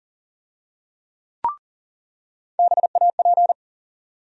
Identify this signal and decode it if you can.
Morse code, heard as a keyed tone, about 40 wpm, 700 Hz. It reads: BAP
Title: USA Call Sign Postfixes - 3 Letters 40wpm